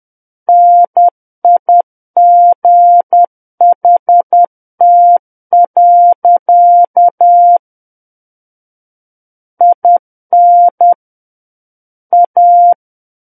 Morse code, heard as a keyed tone, about 10 wpm, 700 Hz. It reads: NIGHT. IN A
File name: Book One / 01-Chapter 1 - 10 WPM